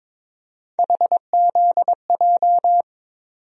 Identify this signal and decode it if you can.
Morse code, heard as a keyed tone, about 22 wpm, 700 Hz. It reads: HZJ
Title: USA Call Sign Postfixes - 3 Letters 22wpm